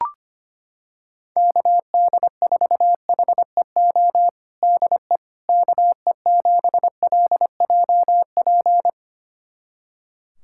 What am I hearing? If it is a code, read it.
Morse code, heard as a keyed tone, about 25 wpm, 700 Hz. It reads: KD4HEO DE KE7LJP
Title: USA Callsign DE Callsign 25wpm